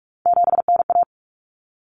Morse code, heard as a keyed tone, about 44 wpm, 700 Hz. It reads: 7NA
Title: Common Cut Numbers - 3-Digit 45wpm